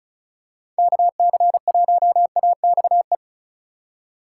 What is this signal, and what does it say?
Morse code, heard as a keyed tone, about 34 wpm, 700 Hz. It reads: KC1AXE
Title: USA Call Signs - 2x Character Spacing - 35wpm